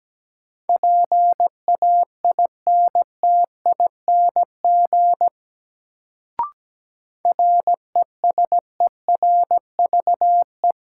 Morse code, heard as a keyed tone, about 17 wpm, 700 Hz. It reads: PAINTING RESERVE
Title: Sentences from Top 1000 Words - Review Part 6 - New Word Drill - 17wpm